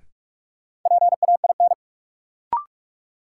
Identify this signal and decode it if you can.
Morse code, heard as a keyed tone, about 46 wpm, 700 Hz. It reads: PAIN
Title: Sentences from Top 1600 Words - New Word Drill - 45wpm